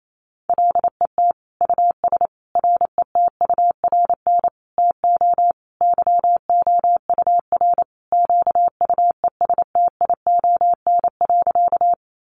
Morse code, heard as a keyed tone, about 28 wpm, 700 Hz. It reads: LET US RETURN TO YOUR QUESTION.
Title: Sentences from Top 800 Words 28wpm